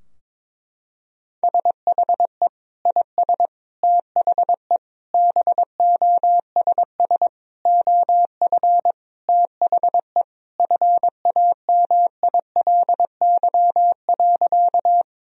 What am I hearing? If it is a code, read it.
Morse code, heard as a keyed tone, about 22 wpm, 700 Hz. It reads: SHE IS THE BOSS OF THE FAMILY.
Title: Sentences from Top 1000 Words - Review Part 4 - 22wpm